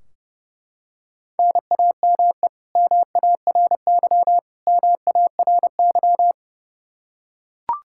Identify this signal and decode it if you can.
Morse code, heard as a keyed tone, about 29 wpm, 700 Hz. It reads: NAME MARY MARY
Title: K1USN SST QSO Practice 30wpm